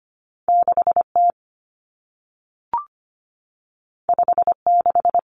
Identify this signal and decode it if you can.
Morse code, heard as a keyed tone, about 25 wpm, 700 Hz. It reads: 6T 56
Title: Common Cut Numbers - 2-Digit 25wpm